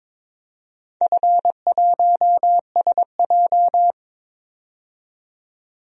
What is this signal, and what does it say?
Morse code, heard as a keyed tone, about 22 wpm, 700 Hz. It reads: F1SJ
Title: DXCC-100 Call Signs - 4 Characters - 22wpm